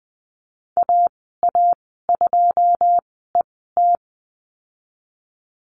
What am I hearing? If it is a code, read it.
Morse code, heard as a keyed tone, about 20 wpm, spaced as for 15 wpm, 700 Hz. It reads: AA2ET
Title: 088 Lesson 15g - Number 5 - Call Signs 20wpm